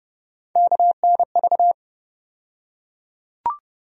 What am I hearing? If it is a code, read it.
Morse code, heard as a keyed tone, about 32 wpm, 700 Hz. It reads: KNV
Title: USA Call Sign Postfixes - 3 Letters 30wpm